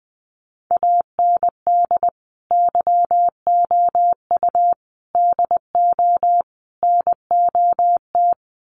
Morse code, heard as a keyed tone, about 20 wpm, 700 Hz. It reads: AND YOU DO NOT
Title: Sets of 4 Words 20wpm